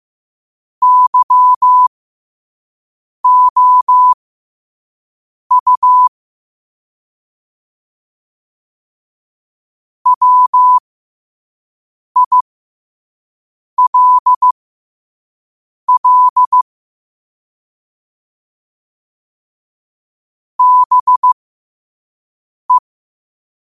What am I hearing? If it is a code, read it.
Morse code, heard as a keyed tone, about 15 wpm, spaced as for 5 wpm, 1000 Hz. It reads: YOU WILL BE